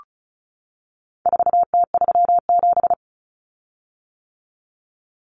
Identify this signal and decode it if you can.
Morse code, heard as a keyed tone, about 34 wpm, 700 Hz. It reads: 4T37
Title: Common Cut Numbers - 4-Digit 35wpm